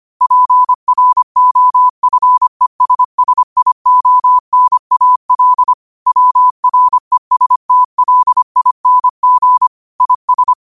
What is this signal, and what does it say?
Morse code, heard as a keyed tone, about 25 wpm, 1000 Hz. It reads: PROFESSIONAL WRESTLING IS
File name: qotdfiles / 25wpm